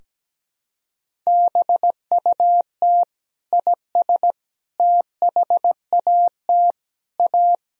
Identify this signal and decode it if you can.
Morse code, heard as a keyed tone, about 17 wpm, 700 Hz. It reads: BUT IS THAT A
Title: Sentences from Top 1000 Words - Review Part 3 - 17wpm